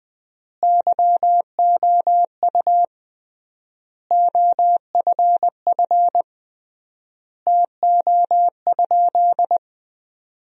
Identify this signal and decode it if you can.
Morse code, heard as a keyed tone, about 21 wpm, 700 Hz. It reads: YOU OFF TO?
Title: 3x Word-Spacing - Sentences from Top 200 Words - 20wpm